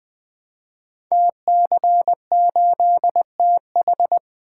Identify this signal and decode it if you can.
Morse code, heard as a keyed tone, about 20 wpm, 700 Hz. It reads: TC8TH
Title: Code Groups - 5 Letter:Numbers - 20wpm